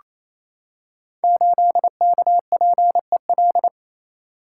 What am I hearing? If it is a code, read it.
Morse code, heard as a keyed tone, about 27 wpm, 700 Hz. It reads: 8KPEL
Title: Code Groups - 5 Letter:Numbers - 28wpm